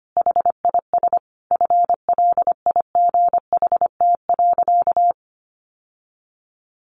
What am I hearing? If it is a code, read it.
Morse code, heard as a keyed tone, about 25 wpm, 700 Hz. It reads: HIS FLIGHT.
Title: Sentences from Top 1400 Words 25wpm